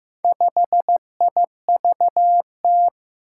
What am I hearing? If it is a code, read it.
Morse code, heard as a keyed tone, about 15 wpm, 700 Hz. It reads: 5IVT
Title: Four Challenging Letter-Number Combinations 15wpm